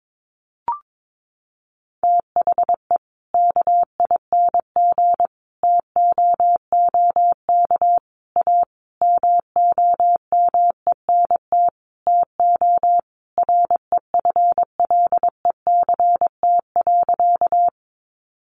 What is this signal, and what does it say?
Morse code, heard as a keyed tone, about 22 wpm, 700 Hz. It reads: THE KING TOOK A MOMENT TO REFLECT.
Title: Sentences from Top 1500 Words 22wpm